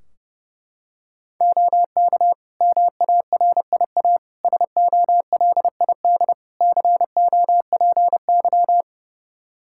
Morse code, heard as a keyed tone, about 30 wpm, 700 Hz. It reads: OK MARIA SOLID COPY
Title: Practice QSO - Exchange 3 - More Info - 30wpm